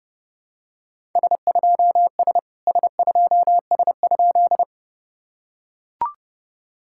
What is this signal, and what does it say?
Morse code, heard as a keyed tone, about 30 wpm, 700 Hz. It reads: S2S S2S?
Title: SOTA QSO Practice 30wpm